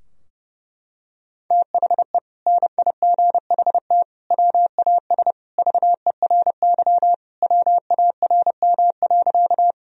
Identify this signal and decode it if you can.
Morse code, heard as a keyed tone, about 32 wpm, 700 Hz. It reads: THE NIGHT WAS VERY WARM.
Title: Sentences from Top 1000 Words - Review Part 2 - 30wpm